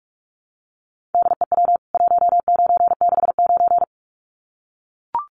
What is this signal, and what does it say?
Morse code, heard as a keyed tone, about 44 wpm, 700 Hz. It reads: DEW 1969
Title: CWT Exchange 45wpm